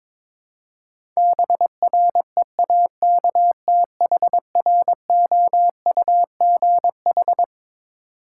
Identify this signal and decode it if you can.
Morse code, heard as a keyed tone, about 22 wpm, 700 Hz. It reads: BREAKTHROUGH
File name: Twelve Letter Words - 22wpm